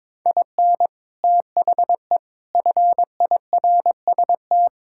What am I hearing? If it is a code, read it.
Morse code, heard as a keyed tone, about 22 wpm, 700 Hz. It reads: IN THE FIRST
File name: practice / 22wpm / Sets of 7 Words - 22wpm